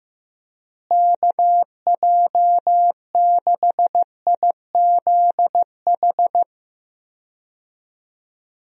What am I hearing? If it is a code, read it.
Morse code, heard as a keyed tone, about 15 wpm, 700 Hz. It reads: KJ6IZH
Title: USA Call Signs - 6 Characters - Repeated 2x - 15wpm